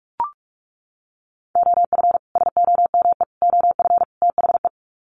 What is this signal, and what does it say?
Morse code, heard as a keyed tone, about 44 wpm, 700 Hz. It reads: OF SOME OF THE
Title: Sets of 4 Words 45wpm